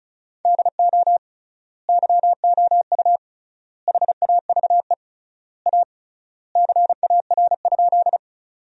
Morse code, heard as a keyed tone, about 36 wpm, 700 Hz. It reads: DO YOU HAVE A CAR?
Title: 3x Word-Spacing - Sentences from Top 300 Words - 35wpm